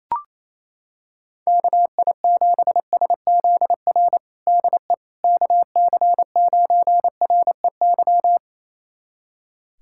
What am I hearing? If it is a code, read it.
Morse code, heard as a keyed tone, about 28 wpm, 700 Hz. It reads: KI7SZR DE KC9REY
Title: USA Callsign DE Callsign - 6 Characters - 28wpm